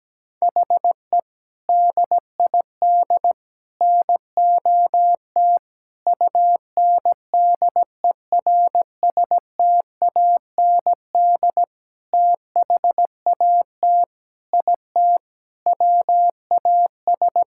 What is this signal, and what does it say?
Morse code, heard as a keyed tone, about 17 wpm, 700 Hz. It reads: HE DID NOT UNDERSTAND THAT IT WAS
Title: Sentences from Top 1000 Words - Review Part 5 - 17wpm